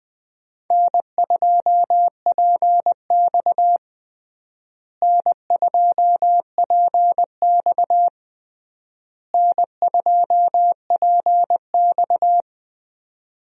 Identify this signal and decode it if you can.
Morse code, heard as a keyed tone, about 20 wpm, 700 Hz. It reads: N2PX N2PX N2PX
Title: USA Call Signs - 4 Characters - Repeated 3x - 20wpm